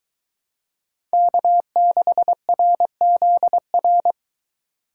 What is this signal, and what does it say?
Morse code, heard as a keyed tone, about 22 wpm, 700 Hz. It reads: K6RZR